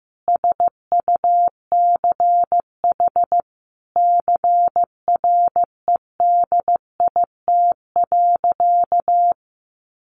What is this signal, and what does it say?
Morse code, heard as a keyed tone, about 15 wpm, 700 Hz. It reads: SUCH CREDIT.
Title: Sentences from Top 1000 Words 15wpm